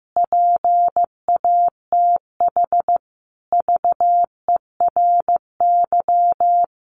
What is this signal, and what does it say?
Morse code, heard as a keyed tone, about 15 wpm, 700 Hz. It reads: PATH VERY
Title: Sentences from Top 1400 Words 15wpm